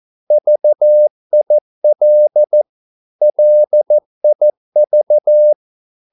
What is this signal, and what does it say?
Morse code, heard as a keyed tone, about 14 wpm, 600 Hz. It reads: VIL LIV
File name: Korte ord DK 14wpm